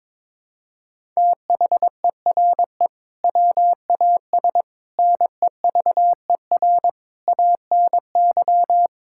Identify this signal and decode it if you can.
Morse code, heard as a keyed tone, about 22 wpm, 700 Hz. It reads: THERE WAS NEVER ANY
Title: Sentences from Top 1000 Words - Review Part 6 - 22wpm